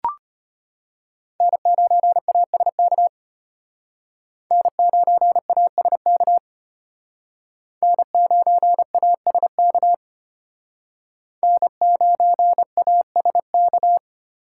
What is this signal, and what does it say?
Morse code, heard as a keyed tone, about 32 wpm, 700 Hz. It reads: N9ASK N9ASK N9ASK N9ASK